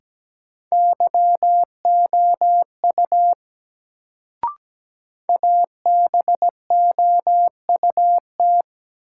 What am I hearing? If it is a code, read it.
Morse code, heard as a keyed tone, about 17 wpm, 700 Hz. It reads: YOU ABOUT
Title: Sentences from Top 100 Words - New Word Drill - 17wpm